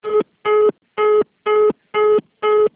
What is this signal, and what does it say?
Congestion tone (425 Hz).
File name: out- -2001- - - .7734